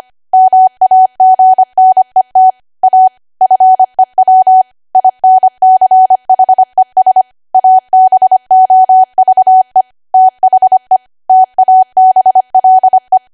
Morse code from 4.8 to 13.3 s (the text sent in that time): INCHES ABOVE THE TABLE